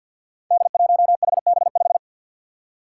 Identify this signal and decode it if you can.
Morse code, heard as a keyed tone, about 47 wpm, 700 Hz. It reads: D1HB5
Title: Five Challenging Letter-Number Combinations 50wpm